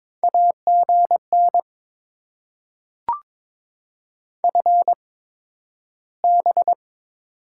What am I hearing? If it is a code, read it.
Morse code, heard as a keyed tone, about 22 wpm, spaced as for 12 wpm, 700 Hz. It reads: AGN FB
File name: Top 20 QSO Elements - 8x Character Spacing - 22wpm